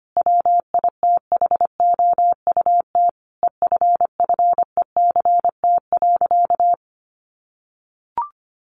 Morse code, heard as a keyed tone, about 25 wpm, 700 Hz. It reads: WITHOUT EFFECT.
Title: Sentences from Top 1400 Words 25wpm